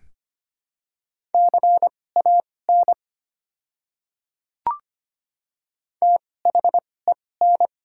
Morse code, heard as a keyed tone, about 25 wpm, spaced as for 18 wpm, 700 Hz. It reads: CAN THEN
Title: 2x Character Spacing - Top 100 Words - 25wpm